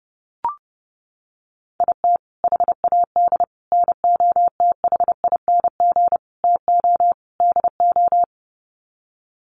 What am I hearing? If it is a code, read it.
Morse code, heard as a keyed tone, about 30 wpm, 700 Hz. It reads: IT HAD NOTHING TO DO
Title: Sets of 5 Words Encore2 30wpm